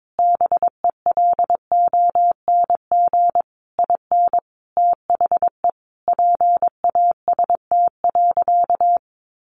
Morse code, heard as a keyed tone, about 22 wpm, 700 Hz. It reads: BELONG IN THE PAST.
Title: A Sound Of Thunder 22wpm